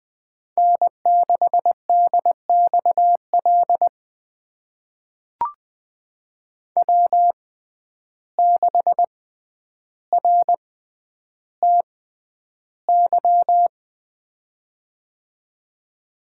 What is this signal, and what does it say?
Morse code, heard as a keyed tone, about 20 wpm, spaced as for 10 wpm, 700 Hz. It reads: N6DXL W6RTY
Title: USA Call Signs - 5 Characters - 6x Character Spacing - 20wpm